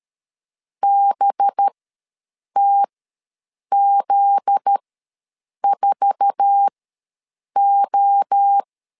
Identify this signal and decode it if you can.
Morse code, heard as a keyed tone, about 13 wpm, spaced as for 7 wpm, 800 Hz. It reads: BTZ4O
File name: LEZIONI / 8 YJ45 / 8 YJ45 ws 12wpm 3146